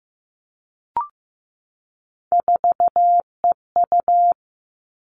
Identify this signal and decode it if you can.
Morse code, heard as a keyed tone, about 15 wpm, 700 Hz. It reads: 4EU
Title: Three Challenging Letter-Number Combinations 15wpm